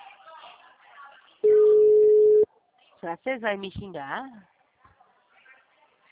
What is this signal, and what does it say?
Ringback tone (Europe, 425 Hz), with speech.
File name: out- -2008- - - .32181